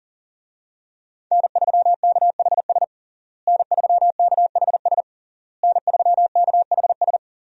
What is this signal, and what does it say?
Morse code, heard as a keyed tone, about 37 wpm, 700 Hz. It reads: N3KHS N3KHS N3KHS
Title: USA Call Signs - 5 Characters - Repeated 3x - 40wpm